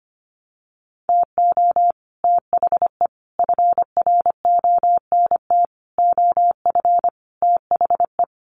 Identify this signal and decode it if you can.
Morse code, heard as a keyed tone, about 25 wpm, 700 Hz. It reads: TO THE FRONT OF THE
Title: Sets of 5 Words 25wpm